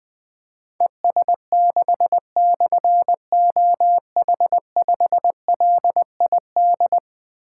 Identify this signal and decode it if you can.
Morse code, heard as a keyed tone, about 20 wpm, 700 Hz. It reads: ES6/OH5LID
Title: Difficult Call Signs 20wpm